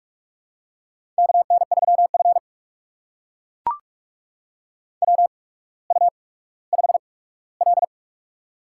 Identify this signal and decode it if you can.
Morse code, heard as a keyed tone, about 44 wpm, spaced as for 19 wpm, 700 Hz. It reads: KN3F WU5L
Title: USA Call Signs - 4 Characters - 8x Character Spacing - 45wpm